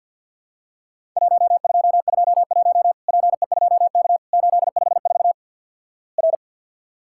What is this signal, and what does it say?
Morse code, heard as a keyed tone, about 48 wpm, 700 Hz. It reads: 1221 PE2K 854 R
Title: Worked All Europe DX Contest - QTC Exchange with Repeats - 50wpm